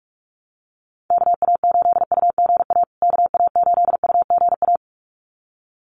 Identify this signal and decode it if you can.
Morse code, heard as a keyed tone, about 46 wpm, 700 Hz. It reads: KA8UGA KA8UGA
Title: USA Call Signs Repeated 2x 45wpm